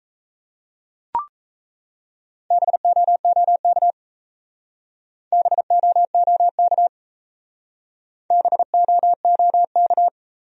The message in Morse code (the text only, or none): BOOK BOOK BOOK